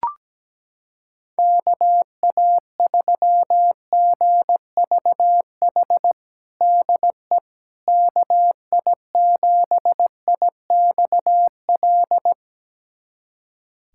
Morse code, heard as a keyed tone, about 17 wpm, 700 Hz. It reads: KA3GVH DE KI7IXL
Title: USA Callsign DE Callsign - 6 Characters - 17wpm